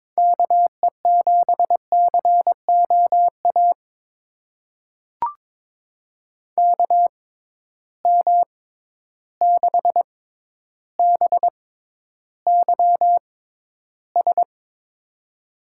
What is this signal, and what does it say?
Morse code, heard as a keyed tone, about 22 wpm, spaced as for 11 wpm, 700 Hz. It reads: KE7COA KM6BYS
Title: USA Call Signs - 6 Characters - 6x Character Spacing - 22wpm